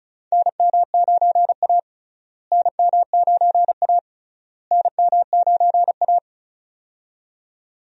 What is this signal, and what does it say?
Morse code, heard as a keyed tone, about 34 wpm, 700 Hz. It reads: NM9A NM9A NM9A
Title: USA Call Signs - 4 Characters - Repeated 3x - 35wpm